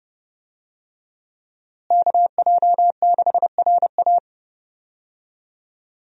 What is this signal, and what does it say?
Morse code, heard as a keyed tone, about 30 wpm, 700 Hz. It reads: KJ6RA
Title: USA Call Signs - 5 Characters - Repeated 3x - 30wpm